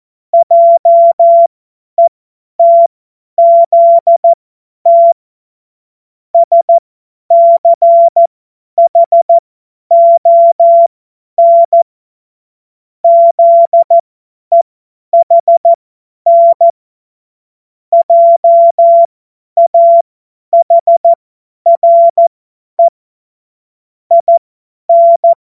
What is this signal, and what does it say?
Morse code, heard as a keyed tone, about 14 wpm, spaced as for 10 wpm, 680 Hz. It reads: JETZT SCHON ZEHN JAHRE IN
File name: Morsekurs / Witze / Witz 08. August 2023 50 F70